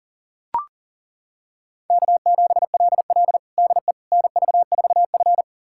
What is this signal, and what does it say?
Morse code, heard as a keyed tone, about 40 wpm, 700 Hz. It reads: K7LL DE NV4F
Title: USA Callsign DE Callsign - 4 Characters - 40wpm